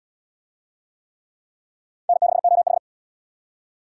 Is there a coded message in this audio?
Yes: Morse code from 2.1 to 2.8 s, heard as a keyed tone, about 65 wpm, 700 Hz. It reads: N6OH